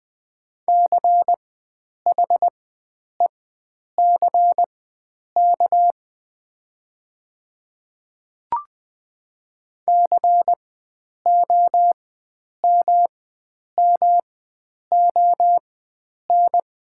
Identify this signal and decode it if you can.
Morse code, heard as a keyed tone, about 20 wpm, spaced as for 9 wpm, 700 Hz. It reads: CHECK COMMON
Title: 4x Character Spacing - Top 500 Words - 20wpm